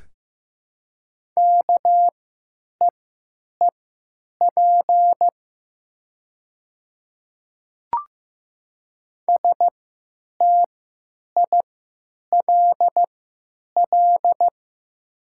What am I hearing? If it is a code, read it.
Morse code, heard as a keyed tone, about 15 wpm, spaced as for 9 wpm, 700 Hz. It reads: KEEP STILL
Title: 3x Character Spacing - Top 500 Words - 15wpm